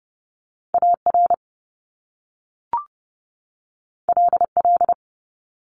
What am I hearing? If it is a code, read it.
Morse code, heard as a keyed tone, about 30 wpm, 700 Hz. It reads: AR LL
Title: Top Two Letter Combinations in Words 30wpm